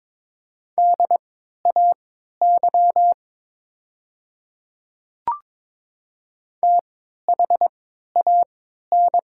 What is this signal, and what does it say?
Morse code, heard as a keyed tone, about 23 wpm, spaced as for 13 wpm, 700 Hz. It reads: DAY THAN
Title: 3x Character Spacing - Top 100 Words - 22wpm